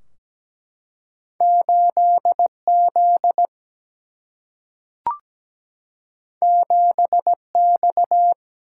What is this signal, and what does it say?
Morse code, heard as a keyed tone, about 17 wpm, 700 Hz. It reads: 8 Z 7 X
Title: Single-Letter Single-Number 17wpm